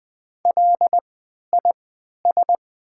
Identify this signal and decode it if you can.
Morse code, heard as a keyed tone, about 20 wpm, spaced as for 11 wpm, 700 Hz. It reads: LIS
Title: 3x Character Spacing - Top 500 Words - 20wpm